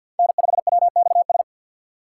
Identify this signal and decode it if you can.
Morse code, heard as a keyed tone, about 48 wpm, 700 Hz. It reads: N5WXS
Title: USA Call Signs - 6x Character Spacing - 50wpm